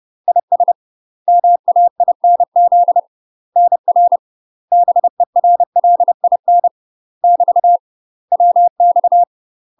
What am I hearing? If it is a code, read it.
Morse code, heard as a keyed tone, about 30 wpm, 700 Hz. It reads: IS MAINZ NR BERLIN <BT> WX
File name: UK MASTER QSOs TRAINING / 30WPM / UK MASTER QSOs 0048